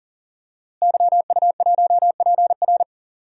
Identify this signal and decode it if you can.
Morse code, heard as a keyed tone, about 40 wpm, 700 Hz. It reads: YU1PR
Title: DXCC-100 Call Signs - 5 Characters - Repeated 2x - 40wpm